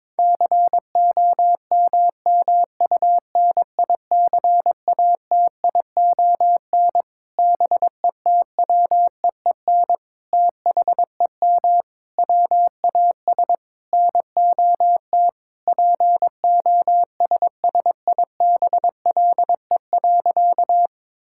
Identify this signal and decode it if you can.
Morse code, heard as a keyed tone, about 22 wpm, 700 Hz. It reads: COMMUNICATION BETWEEN THEM WAS NOT POSSIBLE.
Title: Sentences from Top 1000 Words - Review Part 6 - 22wpm